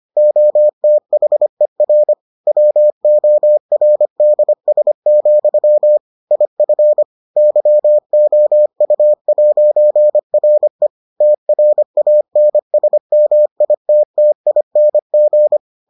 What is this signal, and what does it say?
Morse code, heard as a keyed tone, about 25 wpm, 600 Hz. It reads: OTHER WORDS, IF YOU'RE TRANSMITTING